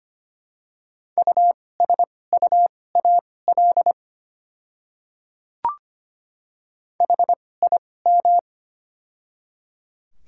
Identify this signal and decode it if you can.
Morse code, heard as a keyed tone, about 25 wpm, spaced as for 18 wpm, 700 Hz. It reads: USUAL HIM
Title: 2x Character Spacing - Top 500 Words - 25wpm